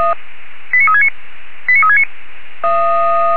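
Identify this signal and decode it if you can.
Five-tone selective call (ZVEI), heard as quick runs of five tones, tones of about 70 ms. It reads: 89369 89369
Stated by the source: Five-tone ZVEI selective call received by SDR.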